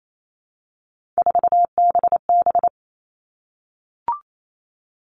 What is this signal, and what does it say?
Morse code, heard as a keyed tone, about 28 wpm, 700 Hz. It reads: 4BB
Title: All Cut Numbers - 3-Digit 28wpm